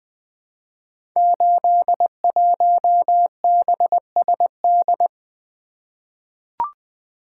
Morse code, heard as a keyed tone, about 20 wpm, 700 Hz. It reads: 81BSD
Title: Code Groups - 5 Letter:Numbers - 20wpm